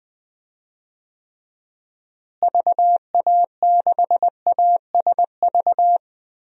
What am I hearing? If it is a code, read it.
Morse code, heard as a keyed tone, about 20 wpm, 700 Hz. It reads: VA6ASV
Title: DXCC-100 Call Signs - 6 Characters - Repeated 2x - 20wpm